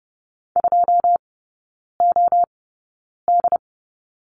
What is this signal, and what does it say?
Morse code, heard as a keyed tone, about 30 wpm, spaced as for 9 wpm, 700 Hz. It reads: 2OD
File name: Lesson 18e - Letter m - Three Characters 30wpm